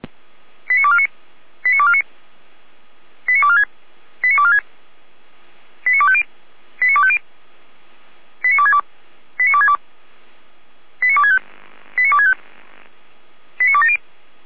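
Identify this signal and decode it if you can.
Five-tone selective call (ZVEI), heard as quick runs of five tones, tones of about 70 ms. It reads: 89249 89249 89246 89246 89250 89250 89262 89262 89265 89265 89270
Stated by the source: ZVEI five-tone selective call received by SDR, hosted by feuerwehr-orlamuende (Orlamünde fire brigade).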